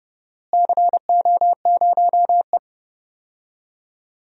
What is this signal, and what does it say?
Morse code, heard as a keyed tone, about 30 wpm, 700 Hz. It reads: CO0E
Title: DXCC-100 Call Signs - 4 Characters - Repeated 2x - 30wpm